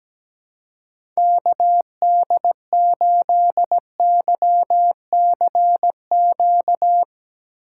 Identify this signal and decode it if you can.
Morse code, heard as a keyed tone, about 17 wpm, 700 Hz. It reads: KD8YCQ
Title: USA Call Signs - 2x Character Spacing - 17wpm